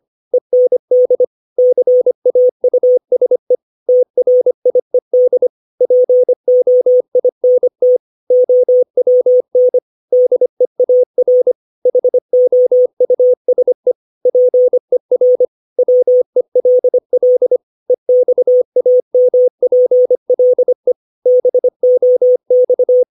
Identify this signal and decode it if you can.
Morse code, heard as a keyed tone, about 25 wpm, 500 Hz. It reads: END CAUSE TRIED POINT OWN DEAR HOUSE PER WELL EXAMPLE BOX
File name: 632 words / 632 Words 25WPM